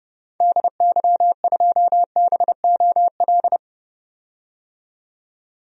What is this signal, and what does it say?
Morse code, heard as a keyed tone, about 30 wpm, 700 Hz. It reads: DY2BOL
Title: DXCC-100 Call Signs - 6 Characters - 30wpm